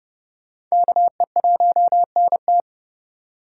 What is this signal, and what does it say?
Morse code, heard as a keyed tone, about 37 wpm, 700 Hz. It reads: KE1NT
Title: Speed Racing - USA 2x2 Call Signs 30wpm